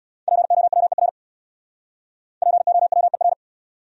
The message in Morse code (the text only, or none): POWER POWER